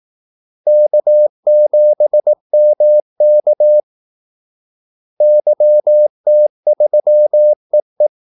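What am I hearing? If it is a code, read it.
Morse code, heard as a keyed tone, about 18 wpm, 600 Hz. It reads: K7MK YT3EE
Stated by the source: Morse call-sign practice at 18 wpm.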